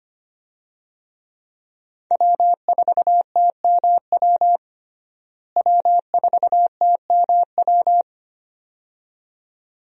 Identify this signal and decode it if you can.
Morse code, heard as a keyed tone, about 25 wpm, 700 Hz. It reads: W4TMW W4TMW
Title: USA Call Signs - 5 Characters - Repeated 2x - 25wpm